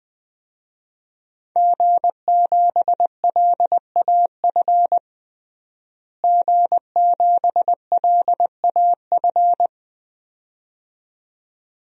Morse code, heard as a keyed tone, about 20 wpm, 700 Hz. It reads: G7LAF G7LAF
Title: DXCC-100 Call Signs - 5 Characters - Repeated 2x - 20wpm